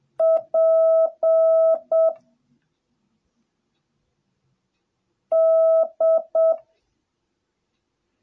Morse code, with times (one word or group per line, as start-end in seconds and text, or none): 0.2-2.1 P
5.3-6.5 D